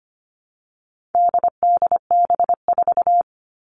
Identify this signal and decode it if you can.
Morse code, heard as a keyed tone, about 25 wpm, 700 Hz. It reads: DDB4
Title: All Cut Numbers - 4-Digit 25wpm